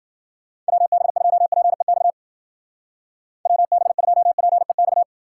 Morse code, heard as a keyed tone, about 48 wpm, 700 Hz. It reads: WB2PEX WB2PEX
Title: Speed Racing - USA 2x3 Call Signs 40wpm